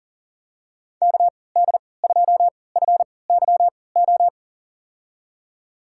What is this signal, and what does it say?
Morse code, heard as a keyed tone, about 40 wpm, spaced as for 23 wpm, 700 Hz. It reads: KD2FYO
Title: USA Call Signs - 6 Characters - 3x Character Spacing - 40wpm